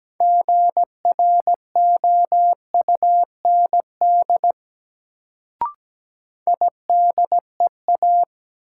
Morse code, heard as a keyed tone, about 17 wpm, 700 Hz. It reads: GROUND IDEA
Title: Sets of 3 Words - New Word Drill - 17wpm